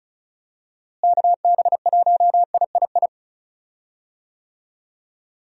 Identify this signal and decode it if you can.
Morse code, heard as a keyed tone, about 36 wpm, 700 Hz. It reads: KB1III
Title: USA Call Signs - 6 Characters - Repeated 3x - 35wpm